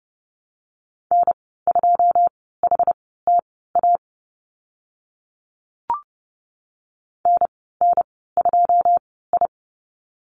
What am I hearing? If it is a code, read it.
Morse code, heard as a keyed tone, about 30 wpm, spaced as for 17 wpm, 700 Hz. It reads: N2HTA NN2I
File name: Lesson 14g - Number 2 - Call Signs 30wpm